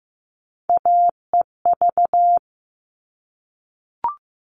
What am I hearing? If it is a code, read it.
Morse code, heard as a keyed tone, about 15 wpm, 700 Hz. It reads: AEV
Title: All Cut Numbers - 3-Digit 15wpm